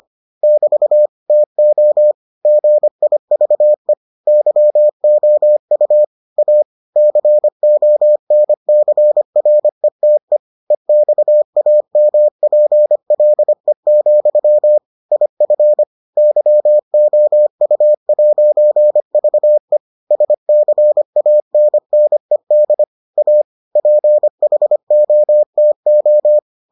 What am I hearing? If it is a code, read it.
Morse code, heard as a keyed tone, about 25 wpm, 600 Hz. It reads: <BT> TO GIVE YOU A CONCRETE EXAMPLE, IF YOU'VE SCANNED A PHOTO